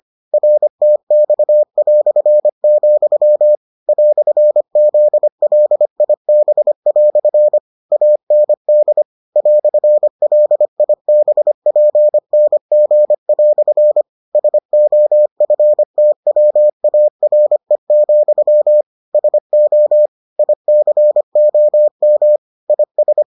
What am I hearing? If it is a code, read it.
Morse code, heard as a keyed tone, about 25 wpm, 600 Hz. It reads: RTX", "ZLIB" AND "LIBPNG" SOFTWARE, SO ICOM IS